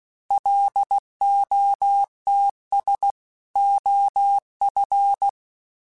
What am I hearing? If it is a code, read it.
Morse code, heard as a keyed tone, about 16 wpm, 800 Hz. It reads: LOTS OF
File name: Weather Sets / Weather Sets-mp3s / Weather 5 12 @15